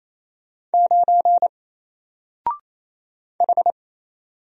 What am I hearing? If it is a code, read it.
Morse code, heard as a keyed tone, about 28 wpm, 700 Hz. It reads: 9 H
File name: Single-Letter-Number-28wpm